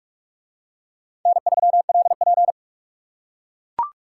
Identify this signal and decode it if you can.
Morse code, heard as a keyed tone, about 44 wpm, 700 Hz. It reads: N3LP